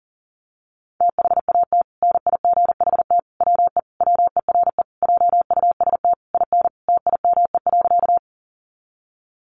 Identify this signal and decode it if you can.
Morse code, heard as a keyed tone, about 40 wpm, 700 Hz. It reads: THAT NIGHT WE WERE JUST IN TIME.
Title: Sentences from Top 300 Words 40wpm